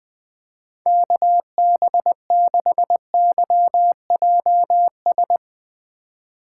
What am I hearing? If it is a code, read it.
Morse code, heard as a keyed tone, about 20 wpm, 700 Hz. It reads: KB6YJS
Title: USA Call Signs - 4x Character Spacing - 20wpm